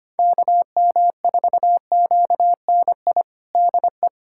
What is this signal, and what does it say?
Morse code, heard as a keyed tone, about 25 wpm, 700 Hz. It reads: KM4QNI DE
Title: USA Callsign DE Callsign - 6 Characters - 25wpm